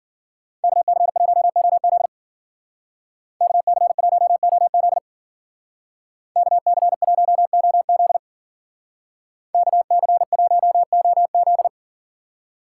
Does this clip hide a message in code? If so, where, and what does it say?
Yes: Morse code from 0.6 to 11.7 s, heard as a keyed tone, about 53 wpm, 700 Hz. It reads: KC1OZ KC1OZ KC1OZ KC1OZ